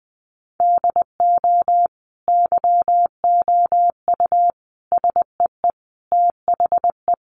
Morse code, heard as a keyed tone, about 19 wpm, 700 Hz. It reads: DO YOU SEE THE
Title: Sentences from Top 1200 Words 20wpm